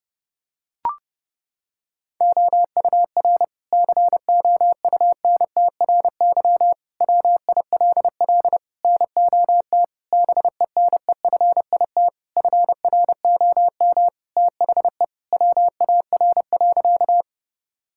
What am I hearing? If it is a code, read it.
Morse code, heard as a keyed tone, about 30 wpm, 700 Hz. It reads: OUR COUNTRY WILL NOT BENEFIT FROM THE WAR.
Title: Sentences from Top 1000 Words - Review Part 6 - 30wpm